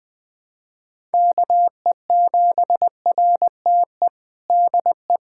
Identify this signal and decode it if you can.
Morse code, heard as a keyed tone, about 20 wpm, 700 Hz. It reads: KE7RTE DE
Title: USA Callsign DE Callsign - 6 Characters - 20wpm